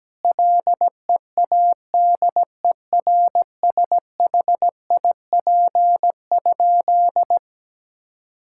Morse code, heard as a keyed tone, about 17 wpm, 700 Hz. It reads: LEADERSHIP?
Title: Sentences from Top 1600 Words 17wpm